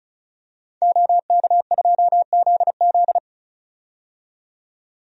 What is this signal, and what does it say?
Morse code, heard as a keyed tone, about 34 wpm, 700 Hz. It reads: OK2ZZ
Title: DXCC-100 Call Signs - 5 Characters - 35wpm